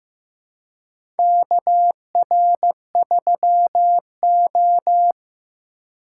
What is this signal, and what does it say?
Morse code, heard as a keyed tone, about 15 wpm, 700 Hz. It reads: KR3O
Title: USA Call Signs - 4 Characters - 2x Character Spacing - 15wpm